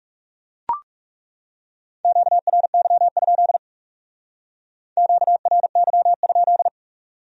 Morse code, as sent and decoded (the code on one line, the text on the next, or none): --.- .-. -.-- ..--.. / --.- .-. -.-- ..--..
QRY? QRY?